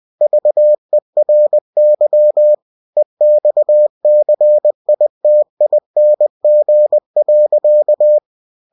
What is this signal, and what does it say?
Morse code, heard as a keyed tone, about 20 wpm, 600 Hz. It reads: VERY EXCITING.
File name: cw-books / winnie-the-pooh / winnie-20wpm-page-0096